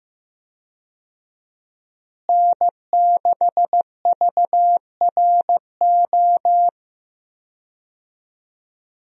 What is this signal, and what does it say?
Morse code, heard as a keyed tone, about 15 wpm, 700 Hz. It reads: N6VRO
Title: USA Call Signs - 5 Characters - Repeated 3x - 15wpm